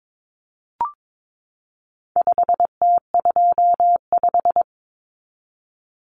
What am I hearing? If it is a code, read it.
Morse code, heard as a keyed tone, about 22 wpm, 700 Hz. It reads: 5T25
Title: Common Cut Numbers - 4-Digit 22wpm